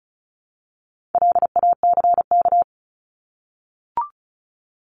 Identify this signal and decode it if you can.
Morse code, heard as a keyed tone, about 34 wpm, 700 Hz. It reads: LACK
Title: Top 1001-1250 Words 35wpm